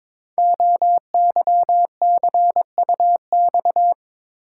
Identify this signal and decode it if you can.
Morse code, heard as a keyed tone, about 22 wpm, 700 Hz. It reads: OYCUX
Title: Code Groups - 5 Letter:Numbers - 22wpm